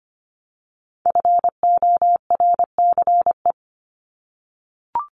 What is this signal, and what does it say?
Morse code, heard as a keyed tone, about 25 wpm, 700 Hz. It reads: FORCE
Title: ED Suffix - Top 100 Words with Base Words 25wpm